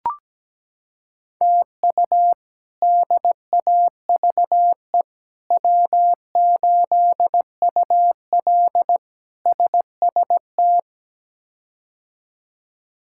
Morse code, heard as a keyed tone, about 17 wpm, 700 Hz. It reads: TU DAVE W8UL SST
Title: K1USN SST QSO Practice - 17wpm